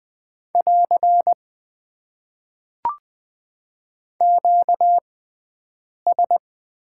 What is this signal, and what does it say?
Morse code, heard as a keyed tone, about 20 wpm, spaced as for 7 wpm, 700 Hz. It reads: <AR> QS
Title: Top 90 QSO Elements - 6x Character Spacing - 20wpm